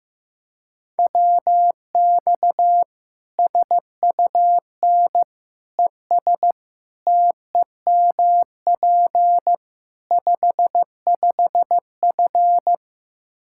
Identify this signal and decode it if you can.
Morse code, heard as a keyed tone, about 15 wpm, 700 Hz. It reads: WX SUN ES TEMP 55F
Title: Practice QSO - Exchange 2 - Info - 15wpm